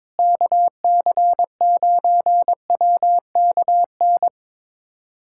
Morse code, heard as a keyed tone, about 21 wpm, 700 Hz. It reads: KC9WKN
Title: USA Callsign DE Callsign 22wpm